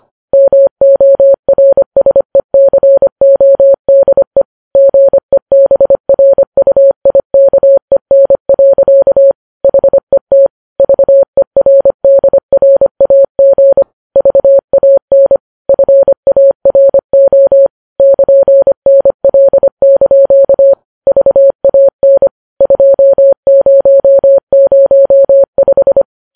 Morse code, heard as a keyed tone, about 25 wpm, 550 Hz. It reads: MORSECODE GEBRUIKEN. HET VERDRAG VAN FARO <KN>NL) VAN 2005
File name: Files Web tls / Morse on9fv / morse25kps